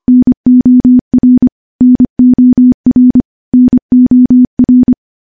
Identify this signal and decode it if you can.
Morse code, heard as a keyed tone, about 25 wpm, 250 Hz. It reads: NOR NOR NOR
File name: morse / morse-training-25WPM